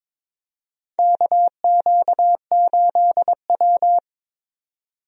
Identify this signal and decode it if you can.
Morse code, heard as a keyed tone, about 21 wpm, 700 Hz. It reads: KQ8W
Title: USA Call Signs - 4 Characters - 22wpm